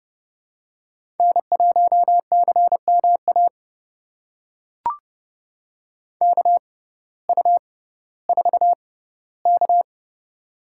Morse code, heard as a keyed tone, about 30 wpm, spaced as for 17 wpm, 700 Hz. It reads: N1CMA KU4K
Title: USA Call Signs - 6x Character Spacing - 30wpm